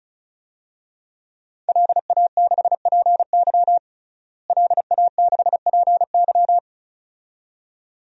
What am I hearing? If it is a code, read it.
Morse code, heard as a keyed tone, about 34 wpm, 700 Hz. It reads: LA6PY LA6PY
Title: DXCC-100 Call Signs - 5 Characters - Repeated 2x - 35wpm